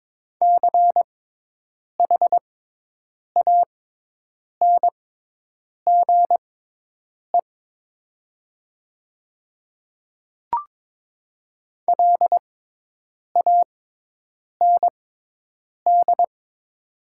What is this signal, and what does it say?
Morse code, heard as a keyed tone, about 22 wpm, spaced as for 8 wpm, 700 Hz. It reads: CHANGE LAND
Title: 6x Character Spacing - Top 200 Words - 22wpm